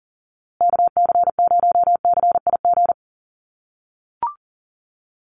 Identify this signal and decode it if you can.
Morse code, heard as a keyed tone, about 40 wpm, 700 Hz. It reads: KC0CIG
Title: USA Call Signs Repeated 2x 40wpm